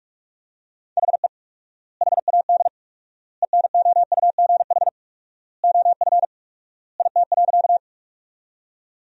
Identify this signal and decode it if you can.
Morse code, heard as a keyed tone, about 46 wpm, 700 Hz. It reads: HE HAD ENOUGH OF IT.
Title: 4x Word-Spacing - Sentences from Top 400 Words - 45wpm